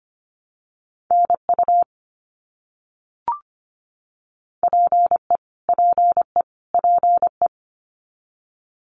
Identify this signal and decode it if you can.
Morse code, heard as a keyed tone, about 25 wpm, 700 Hz. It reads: NU PE PE PE
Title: Canadian Provinces and Territories - Repeated 3x - 25wpm